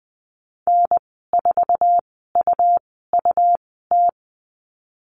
Morse code, heard as a keyed tone, about 20 wpm, spaced as for 15 wpm, 700 Hz. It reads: N4UUT
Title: 154 Lesson 25g - Letter p - Call Signs 20wpm